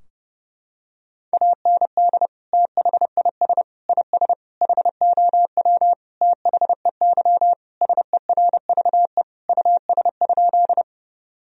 Morse code, heard as a keyed tone, about 30 wpm, 700 Hz. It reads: AND THIS IS HOW THEY SERVE US?
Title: Sentences from Top 1000 Words - Review Part 2 - 30wpm